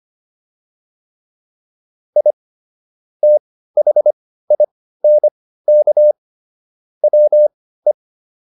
Morse code, heard as a keyed tone, about 25 wpm, spaced as for 15 wpm, 600 Hz. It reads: I THINK WE